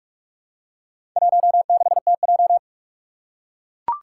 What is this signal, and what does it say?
Morse code, heard as a keyed tone, about 44 wpm, 700 Hz. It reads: 16TJ
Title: Four Challenging Letter-Number Combinations 45wpm